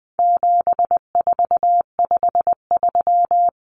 Morse code, heard as a keyed tone, about 20 wpm, 700 Hz. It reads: 7453
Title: Numbers - 4-Digit 20wpm